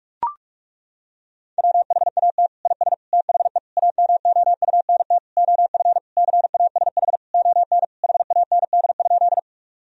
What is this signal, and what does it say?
Morse code, heard as a keyed tone, about 41 wpm, 700 Hz. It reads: WHAT IS THE AMOUNT OF CASH ON HAND?
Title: Sentences from Top 1000 Words - Review Part 4 - 45wpm